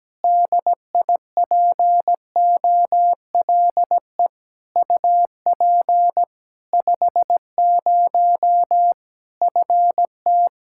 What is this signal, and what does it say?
Morse code, heard as a keyed tone, about 17 wpm, 700 Hz. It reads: DIPOLE UP 50 FT
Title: Practice QSO - Exchange 2 - Info - 17wpm